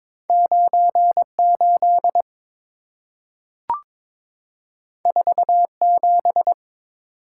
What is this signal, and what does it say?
Morse code, heard as a keyed tone, about 22 wpm, 690 Hz. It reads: 98 47
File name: practice / 22wpm / Two Letter-Number Combinations - 22wpm